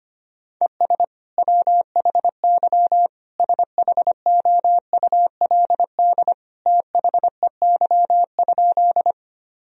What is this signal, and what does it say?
Morse code, heard as a keyed tone, about 25 wpm, 700 Hz. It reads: ES WHY SHOULD THEY?
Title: Sentences from Top 300 Words and Abbreviations 25wpm